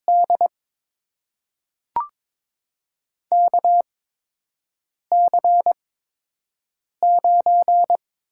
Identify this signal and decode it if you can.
Morse code, heard as a keyed tone, about 22 wpm, spaced as for 6 wpm, 700 Hz. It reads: D KC9
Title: USA Call Signs - 6 Characters - 8x Character Spacing - 22wpm